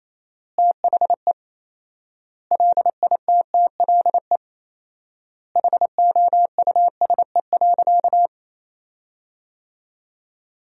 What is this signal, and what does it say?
Morse code, heard as a keyed tone, about 27 wpm, 700 Hz. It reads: THE LITTLE HOUSE.
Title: 4x Word-Spacing - Sentences from Top 200 Words - 28wpm